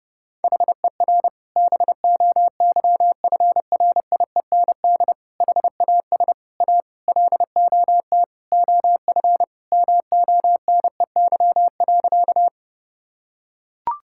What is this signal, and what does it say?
Morse code, heard as a keyed tone, about 30 wpm, 700 Hz. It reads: HER BOYFRIEND HAS A LOT OF MONEY.
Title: Sentences from Top 1000 Words - Review Part 5 - 30wpm